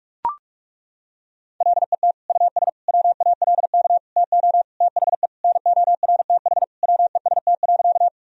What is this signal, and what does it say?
Morse code, heard as a keyed tone, about 41 wpm, 700 Hz. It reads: LET US WALK TO THE NORTH WEST.
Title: Sentences from Top 1000 Words - Review Part 2 - 45wpm